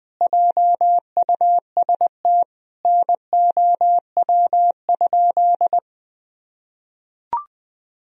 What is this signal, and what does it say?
Morse code, heard as a keyed tone, about 21 wpm, 700 Hz. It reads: JUST NOW?
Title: Sentences from Top 1000 Words - Review Part 3 - 20wpm